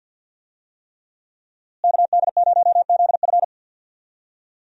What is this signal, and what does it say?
Morse code, heard as a keyed tone, about 49 wpm, 700 Hz. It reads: KD0ZF
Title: USA Call Signs - 5 Characters - Repeated 3x - 50wpm